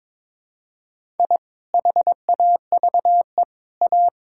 Morse code, heard as a keyed tone, about 22 wpm, 700 Hz. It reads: I HAVE A
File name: practice / 22wpm / Sentences with 3-Word Phrases - 22wpm